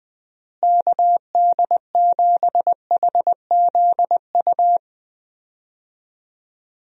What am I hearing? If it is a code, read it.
Morse code, heard as a keyed tone, about 20 wpm, 700 Hz. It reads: KD7HZU
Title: USA Call Signs - 6 Characters - Repeated 3x - 20wpm